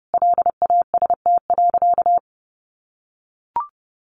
Morse code, heard as a keyed tone, about 30 wpm, 700 Hz. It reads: LAST.
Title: Sentences from Top 1100 Words 30wpm